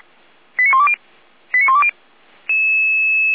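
Five-tone selective call (ZVEI), heard as quick runs of five tones, tones of about 70 ms. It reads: 89120 89120
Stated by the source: ZVEI five-tone call 89340 received with an SDR.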